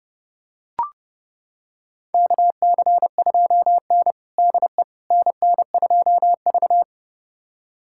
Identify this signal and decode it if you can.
Morse code, heard as a keyed tone, about 30 wpm, 700 Hz. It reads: KC2N DE NN2V
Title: USA Callsign DE Callsign - 4 Characters - 30wpm